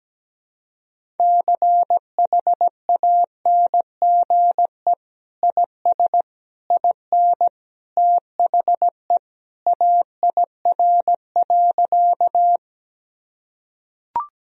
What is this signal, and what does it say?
Morse code, heard as a keyed tone, about 17 wpm, 700 Hz. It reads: CHANGE IS IN THE AIR.
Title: Sentences from Top 1000 Words - Review Part 1 - 17wpm